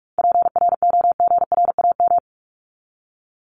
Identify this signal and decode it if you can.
Morse code, heard as a keyed tone, about 45 wpm, 700 Hz. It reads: PROGRAM
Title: Top 501-750 Words 45wpm